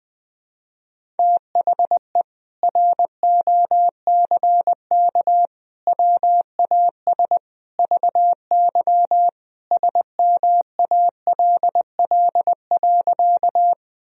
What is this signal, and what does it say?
Morse code, heard as a keyed tone, about 20 wpm, 700 Hz. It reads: THE ROCK WAS VY SMALL.
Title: Sentences from Top 400 Words and Abbreviations 20wpm